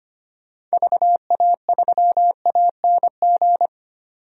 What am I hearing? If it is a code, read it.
Morse code, heard as a keyed tone, about 25 wpm, 690 Hz. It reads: VA3ANG
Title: DXCC-100 Call Signs - 6 Characters - 25wpm